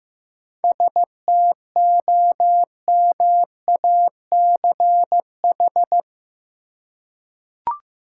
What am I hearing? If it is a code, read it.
Morse code, heard as a keyed tone, about 15 wpm, 700 Hz. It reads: STOMACH
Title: Seven Letter Words - 15wpm